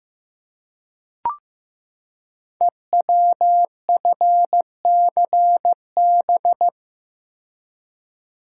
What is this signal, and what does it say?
Morse code, heard as a keyed tone, about 15 wpm, 700 Hz. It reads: EWFCB
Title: Code Groups - 5 Letters - 15wpm